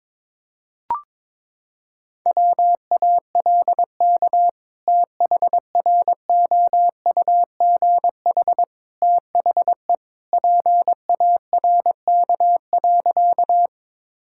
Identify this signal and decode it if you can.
Morse code, heard as a keyed tone, about 22 wpm, 700 Hz. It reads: WALK THROUGH THE PARK.
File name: practice / 22wpm / Sentences with 2-Word Phrases - 22wpm